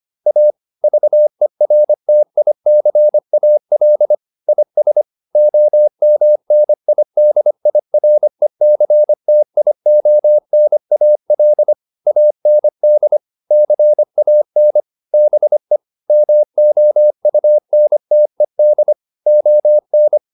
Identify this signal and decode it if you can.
Morse code, heard as a keyed tone, about 25 wpm, 600 Hz. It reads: A VERTICAL IS OMNIDIRECTIONAL AND CAN BE MOUNTED ON